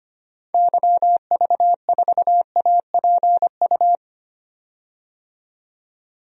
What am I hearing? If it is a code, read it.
Morse code, heard as a keyed tone, about 25 wpm, 700 Hz. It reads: YV4APU
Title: DXCC-100 Call Signs - 6 Characters - 25wpm